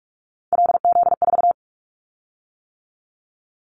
Morse code, heard as a keyed tone, about 46 wpm, 700 Hz. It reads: LZ4
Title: Callsign Prefixes for Top 25 DXCC - 45wpm